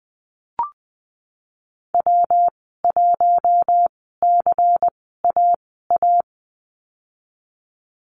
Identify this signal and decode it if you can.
Morse code, heard as a keyed tone, about 20 wpm, spaced as for 15 wpm, 700 Hz. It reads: W1CAA
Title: 167 Lesson 27g - Number 7 - Call Signs 20wpm